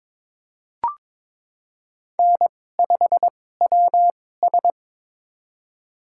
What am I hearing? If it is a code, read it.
Morse code, heard as a keyed tone, about 21 wpm, spaced as for 16 wpm, 700 Hz. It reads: N5WS
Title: USA Call Signs - 4 Characters - 2x Character Spacing - 22wpm